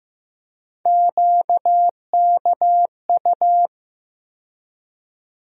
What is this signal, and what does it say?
Morse code, heard as a keyed tone, about 15 wpm, 700 Hz. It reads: QKU
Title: USA Call Sign Postfixes - 3 Letters 15wpm